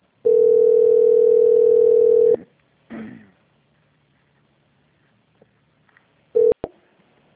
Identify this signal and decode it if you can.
Ringback tone (North America), 440 + 480 Hz.